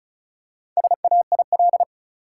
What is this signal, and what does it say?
Morse code, heard as a keyed tone, about 34 wpm, 700 Hz. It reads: SAIL
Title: Four Letter Words - 35wpm